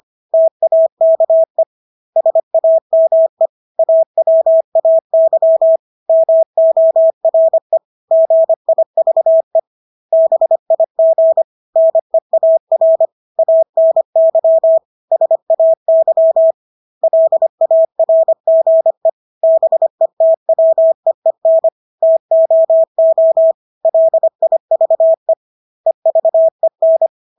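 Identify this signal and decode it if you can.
Morse code, heard as a keyed tone, about 25 wpm, 650 Hz. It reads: TAKE SAME AWAY MORE GIVE BIG NEAR ANY SAY LARGE BETWEEN TOO LIVE EVEN
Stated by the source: Morse practice of English words at 25 wpm from CW med Gnister.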